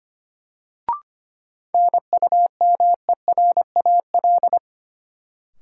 Morse code, heard as a keyed tone, about 25 wpm, 700 Hz. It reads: NUMERAL
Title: Sentences from Top 1000 Words - Review Part 2 - New Word Drill - 25wpm